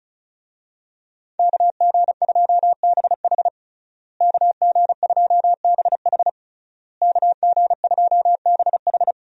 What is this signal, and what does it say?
Morse code, heard as a keyed tone, about 36 wpm, 700 Hz. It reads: KG2BH KG2BH KG2BH
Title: USA Call Signs - 5 Characters - Repeated 3x - 35wpm